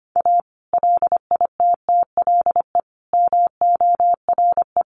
Morse code, heard as a keyed tone, about 25 wpm, 700 Hz. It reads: A LITTLE MORE
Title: Sets of 5 Words Encore2 25wpm